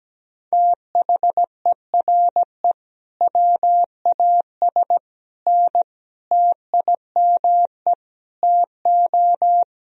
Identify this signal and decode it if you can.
Morse code, heard as a keyed tone, about 17 wpm, 700 Hz. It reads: THERE WAS N TIME TO
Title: Sentences from Top 400 Words and Abbreviations 17wpm